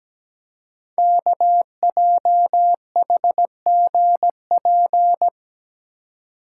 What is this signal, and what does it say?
Morse code, heard as a keyed tone, about 17 wpm, 700 Hz. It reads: KJHGP
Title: Code Groups - 5 Letters - 17wpm